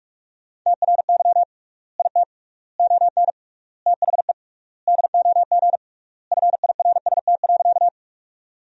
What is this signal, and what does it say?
Morse code, heard as a keyed tone, about 42 wpm, 700 Hz. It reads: TRY IT ON THE DOG FIRST.
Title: 3x Word-Spacing - Sentences from Top 400 Words - 45wpm